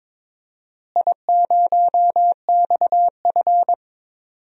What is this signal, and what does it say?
Morse code, heard as a keyed tone, about 22 wpm, 700 Hz. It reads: I0XF
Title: DXCC-100 Call Signs - 4 Characters - Repeated 2x - 22wpm